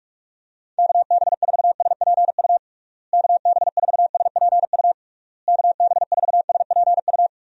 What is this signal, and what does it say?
Morse code, heard as a keyed tone, about 41 wpm, 700 Hz. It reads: KB4SPU KB4SPU KB4SPU
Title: USA Call Signs - 6 Characters - Repeated 3x - 45wpm